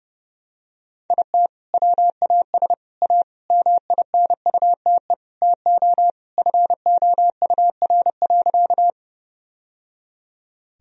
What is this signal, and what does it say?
Morse code, heard as a keyed tone, about 32 wpm, 700 Hz. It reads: IT WAS A MINUTE TO FOUR.
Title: Sentences from Top 1000 Words - Review Part 1 - 30wpm